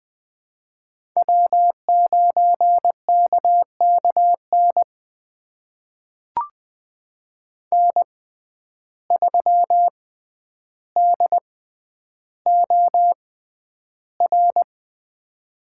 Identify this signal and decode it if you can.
Morse code, heard as a keyed tone, about 20 wpm, spaced as for 10 wpm, 700 Hz. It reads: W9KKN N3DOR
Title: USA Call Signs - 5 Characters - 6x Character Spacing - 20wpm